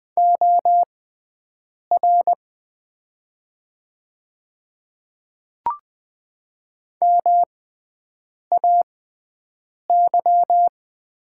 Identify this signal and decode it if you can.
Morse code, heard as a keyed tone, about 20 wpm, spaced as for 7 wpm, 700 Hz. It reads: OR MAY
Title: 6x Character Spacing - Top 100 Words - 20wpm